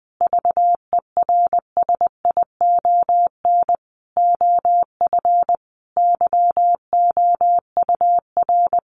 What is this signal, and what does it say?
Morse code, heard as a keyed tone, about 20 wpm, 700 Hz. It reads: VERSION OF YOUR
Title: Sentences from Top 1400 Words 20wpm